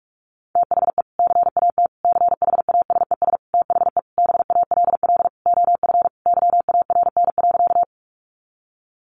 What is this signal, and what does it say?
Morse code, heard as a keyed tone, about 44 wpm, 700 Hz. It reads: THE CAT CHASES THE BALL OF YARN.
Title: Additional Easy Phrases 45wpm